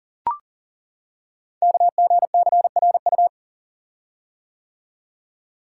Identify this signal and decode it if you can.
Morse code, heard as a keyed tone, about 39 wpm, 690 Hz. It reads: KGCRU
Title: Code Groups - 5 Letters - 40wpm